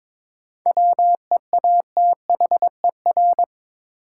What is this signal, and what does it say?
Morse code, heard as a keyed tone, about 21 wpm, 700 Hz. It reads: WEATHER
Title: Sentences from Top 800 Words - New Word Drill - 22wpm